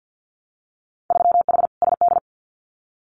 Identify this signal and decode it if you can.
Morse code, heard as a keyed tone, about 50 wpm, 700 Hz. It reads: 3H SD
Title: Winter Field Day Exchanges Encore 50wpm